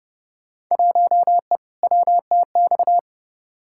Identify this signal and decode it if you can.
Morse code, heard as a keyed tone, about 30 wpm, 700 Hz. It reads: 1E WTX
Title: Field Day Exchanges - 30wpm